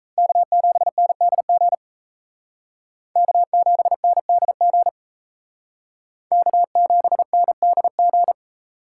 Morse code, heard as a keyed tone, about 36 wpm, 700 Hz. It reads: K7NDG K7NDG K7NDG